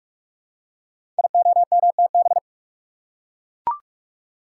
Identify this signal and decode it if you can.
Morse code, heard as a keyed tone, about 44 wpm, 690 Hz. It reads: IOMTB
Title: Code Groups - 5 Letters - 45wpm